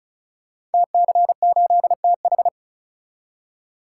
Morse code, heard as a keyed tone, about 36 wpm, 700 Hz. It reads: TC8TH
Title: Code Groups - 5 Letter:Numbers - 35wpm